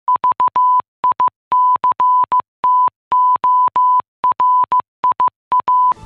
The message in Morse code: VICTORIA

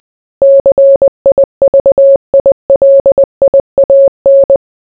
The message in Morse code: CIVILIAN

